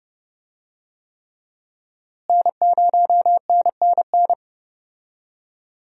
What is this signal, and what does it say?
Morse code, heard as a keyed tone, about 30 wpm, 700 Hz. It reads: N0NNN
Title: USA Call Signs - 5 Characters - Repeated 3x - 30wpm